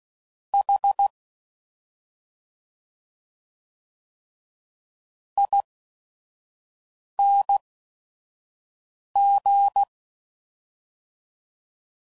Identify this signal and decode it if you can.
Morse code, heard as a keyed tone, about 16 wpm, spaced as for 5 wpm, 800 Hz. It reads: H ING